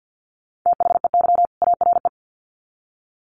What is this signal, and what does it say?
Morse code, heard as a keyed tone, about 50 wpm, 700 Hz. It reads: THEY ARE
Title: Sets of 2 Words 50wpm